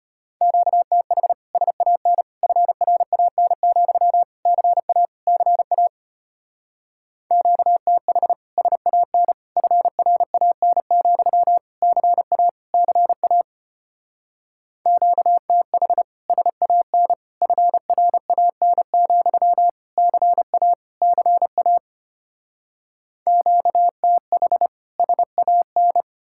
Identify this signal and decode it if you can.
Morse code, heard as a keyed tone, about 33 wpm, 700 Hz. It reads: QTH SAN FRAN, CA CA QTH SAN FRAN, CA CA QTH SAN FRAN, CA CA QTH SAN